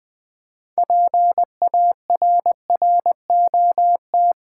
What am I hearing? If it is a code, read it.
Morse code, heard as a keyed tone, about 20 wpm, 700 Hz. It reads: PARROT
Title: Difficult Words with Hints - 20wpm